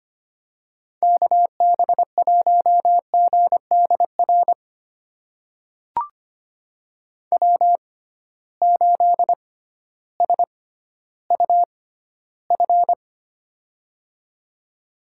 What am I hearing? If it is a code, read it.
Morse code, heard as a keyed tone, about 25 wpm, spaced as for 13 wpm, 700 Hz. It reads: KB1GDR W8SUF
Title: USA Call Signs - 6x Character Spacing - 25wpm